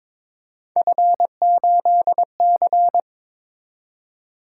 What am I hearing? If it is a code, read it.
Morse code, heard as a keyed tone, about 22 wpm, 700 Hz. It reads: F8C
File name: practice / 22wpm / Three Letter-Number Combinations - 22wpm